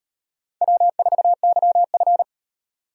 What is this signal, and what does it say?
Morse code, heard as a keyed tone, about 38 wpm, 700 Hz. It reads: W4YF